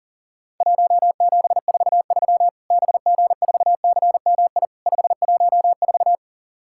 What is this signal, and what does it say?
Morse code, heard as a keyed tone, about 40 wpm, 700 Hz. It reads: 1743 BG4CMI 514
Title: Worked All Europe DX Contest - QTC Exchange - 40wpm